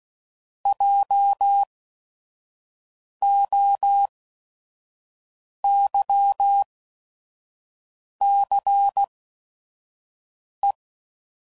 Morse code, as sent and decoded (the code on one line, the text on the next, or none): .--- --- -.-- -.-. .
JOYCE